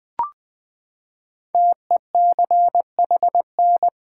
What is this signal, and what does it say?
Morse code, heard as a keyed tone, about 20 wpm, 700 Hz. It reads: TECHN
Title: Build A Word - 5 Letters At A Time - 10-Letter Words - 20wpm